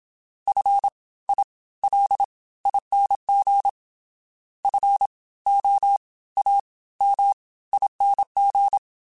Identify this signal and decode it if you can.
Morse code, heard as a keyed tone, about 27 wpm, spaced as for 18 wpm, 800 Hz. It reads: FILING FOAMING